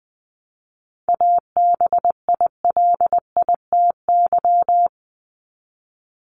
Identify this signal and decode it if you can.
Morse code, heard as a keyed tone, about 20 wpm, 700 Hz. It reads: ABILITY
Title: Top 501-750 Words 20wpm